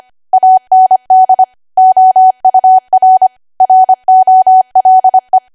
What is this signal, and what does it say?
Morse code, heard as a keyed tone, about 25 wpm, 750 Hz. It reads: AND OUR ROLE
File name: Archive / 25 WPM / 25WPM